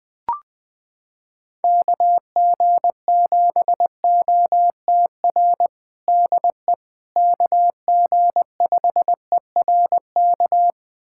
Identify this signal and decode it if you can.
Morse code, heard as a keyed tone, about 20 wpm, 700 Hz. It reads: KG7OTR DE KG5ERK
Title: USA Callsign DE Callsign - 6 Characters - 20wpm